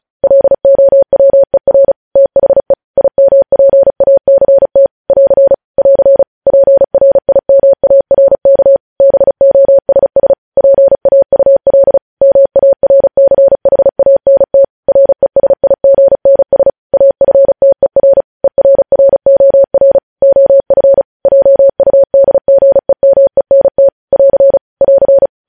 Morse code, heard as a keyed tone, about 36 wpm, 550 Hz. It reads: LOWER THE IMPACT <AR> <AR> PRIMARK BOSS PAUL MARCHANT RESIGNS AFTER ERROR OF JUDGEMENT <AR> <AR>
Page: News headlines in Morse code at 35 wpm